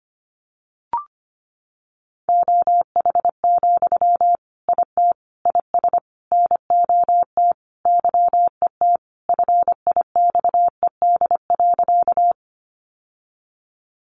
Morse code, heard as a keyed tone, about 25 wpm, 700 Hz. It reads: OH, IT IS NOT YET FIXED.
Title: Sentences from Top 1100 Words 25wpm